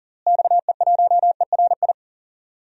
Morse code, heard as a keyed tone, about 40 wpm, 700 Hz. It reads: XE1ERI
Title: DXCC-100 Call Signs - 6 Characters - Repeated 3x - 40wpm